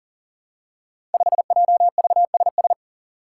Morse code, heard as a keyed tone, about 42 wpm, 700 Hz. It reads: 5JVSS
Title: Five Challenging Letter-Number Combinations 40wpm